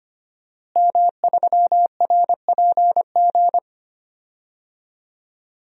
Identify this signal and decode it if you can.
Morse code, heard as a keyed tone, about 25 wpm, 700 Hz. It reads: M3RPG
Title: DXCC-100 Call Signs - 5 Characters - Repeated 2x - 25wpm